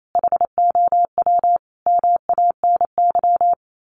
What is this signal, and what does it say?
Morse code, heard as a keyed tone, about 28 wpm, 700 Hz. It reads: HOW MANY
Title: Sets of 5 Words Encore 28wpm